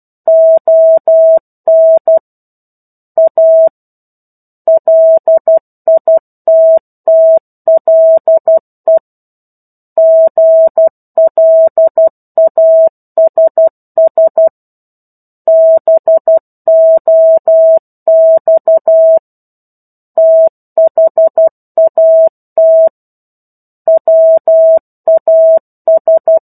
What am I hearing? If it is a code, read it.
Morse code, heard as a keyed tone, about 12 wpm, 650 Hz. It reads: ON A LITTLE GLASS BOX THAT WAS
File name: 01-Chapter-01 12 WPM